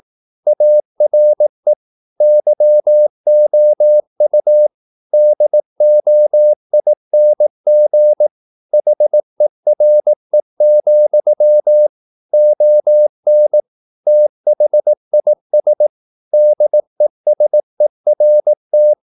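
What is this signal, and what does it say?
Morse code, heard as a keyed tone, about 18 wpm, 600 Hz. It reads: ARE YOU DOING HERE, ON THIS DESERT
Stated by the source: Morse practice at 18 wpm, English text, from CW med Gnister.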